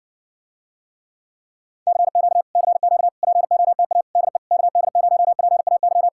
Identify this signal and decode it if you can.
Morse code, heard as a keyed tone, about 60 wpm, 700 Hz. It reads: CQ CQ POTA DE KD0PAY